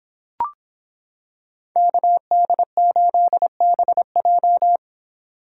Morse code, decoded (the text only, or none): KD8BJ